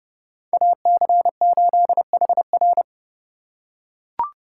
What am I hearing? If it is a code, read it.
Morse code, heard as a keyed tone, about 32 wpm, 700 Hz. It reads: AC8HR